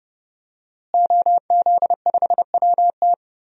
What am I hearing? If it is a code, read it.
Morse code, heard as a keyed tone, about 30 wpm, 700 Hz. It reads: OZ5WT
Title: DXCC-100 Call Signs - 5 Characters - Repeated 3x - 30wpm